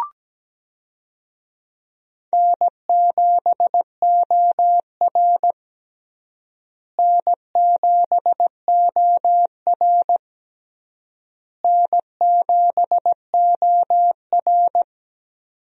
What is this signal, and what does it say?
Morse code, heard as a keyed tone, about 17 wpm, 700 Hz. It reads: N7OR N7OR N7OR
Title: USA Call Signs - 4 Characters - Repeated 3x - 17wpm